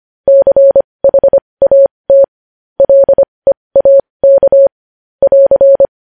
Morse code, heard as a keyed tone, about 25 wpm, spaced as for 20 wpm, 550 Hz. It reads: CHAT LEAK <AR>
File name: podcast / News headlines-1x-25x20